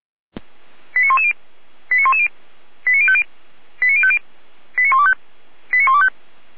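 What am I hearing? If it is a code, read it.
Five-tone selective call (ZVEI), heard as quick runs of five tones, tones of about 70 ms. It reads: 89110 89110 89050 89050 89125 89125
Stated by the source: ZVEI five-tone selective call received on SDR.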